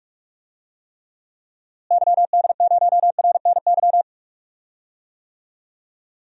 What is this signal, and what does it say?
Morse code, heard as a keyed tone, about 46 wpm, 700 Hz. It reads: YD0RNY
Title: DXCC-100 Call Signs - 6 Characters - Repeated 3x - 45wpm